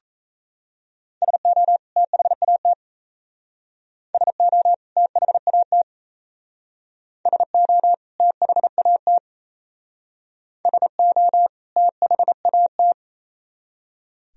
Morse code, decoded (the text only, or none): SO THAT SO THAT SO THAT SO THAT